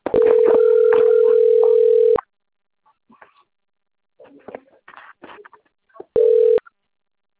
Ringback tone (North America), 440 + 480 Hz.